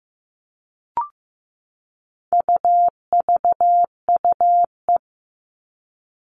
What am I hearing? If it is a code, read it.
Morse code, heard as a keyed tone, about 15 wpm, 700 Hz. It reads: UVUE
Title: All Cut Numbers - 4-Digit 15wpm